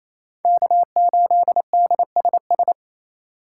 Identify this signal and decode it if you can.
Morse code, heard as a keyed tone, about 28 wpm, 700 Hz. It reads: K8DSS
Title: K1USN SST QSO Practice - 28wpm